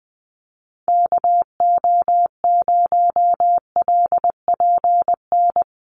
Morse code, heard as a keyed tone, about 19 wpm, 700 Hz. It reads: KO0LPN
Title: USA Call Signs Repeated 3x 20wpm